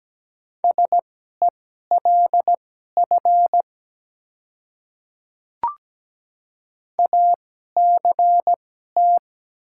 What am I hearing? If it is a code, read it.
Morse code, heard as a keyed tone, about 17 wpm, spaced as for 12 wpm, 700 Hz. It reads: SELF ACT
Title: 2x Character Spacing - Top 200 Words - 17wpm